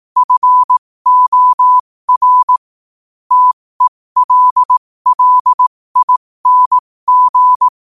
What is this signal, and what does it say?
Morse code, heard as a keyed tone, about 18 wpm, spaced as for 15 wpm, 1000 Hz. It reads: FOR TELLING
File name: qotdfiles / 15wpm